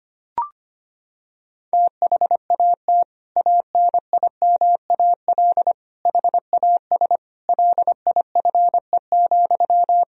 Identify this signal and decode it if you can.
Morse code, heard as a keyed tone, about 25 wpm, 700 Hz. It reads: THAT ANIMAL HAS LIFE,
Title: Sentences from Top 1000 Words - Review Part 1 - 25wpm